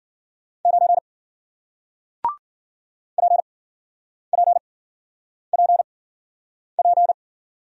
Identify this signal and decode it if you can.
Morse code, heard as a keyed tone, about 56 wpm, spaced as for 9 wpm, 700 Hz. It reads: 9 PPPP